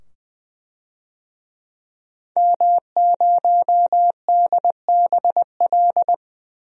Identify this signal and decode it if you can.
Morse code, heard as a keyed tone, about 21 wpm, 700 Hz. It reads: M0DBL
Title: DXCC-100 Call Signs - 5 Characters - 20wpm